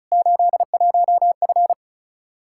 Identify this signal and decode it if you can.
Morse code, heard as a keyed tone, about 36 wpm, 700 Hz. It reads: 81F
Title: Practice QSO - Exchange 2 - Info - 35wpm